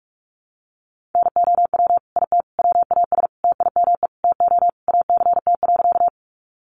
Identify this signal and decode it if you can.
Morse code, heard as a keyed tone, about 44 wpm, 700 Hz. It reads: NOW IT WAS TIME TO ACT.
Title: Sentences from Top 200 Words 45wpm